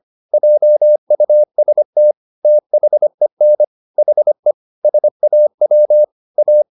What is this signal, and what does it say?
Morse code, heard as a keyed tone, about 25 wpm, 600 Hz. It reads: JUST THEN HE SAW A
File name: the old man and the sea part 1 25wpm